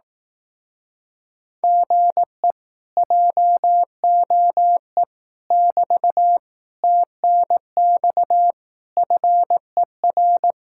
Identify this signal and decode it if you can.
Morse code, heard as a keyed tone, about 18 wpm, 700 Hz. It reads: GE JOE <BT> TNX FER
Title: Speed-Racing Common QSO Phrases Encore - 15wpm